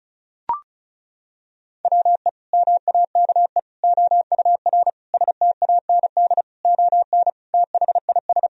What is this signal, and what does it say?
Morse code, heard as a keyed tone, about 34 wpm, 700 Hz. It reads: WE MAKE OUR STAND ON THIS
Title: Sentences from Top 1000 Words - Review Part 2 - 35wpm